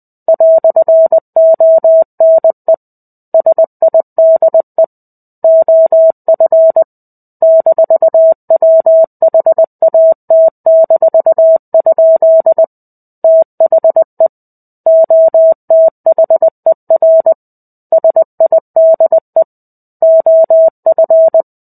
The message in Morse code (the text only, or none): "ONE SIDE OF -WHAT-? THE OTHER SIDE OF